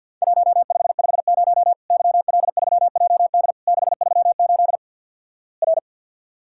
Morse code, heard as a keyed tone, about 51 wpm, 700 Hz. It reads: 1550 YL3JD 638 R
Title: Worked All Europe DX Contest - QTC Exchange with Repeats - 50wpm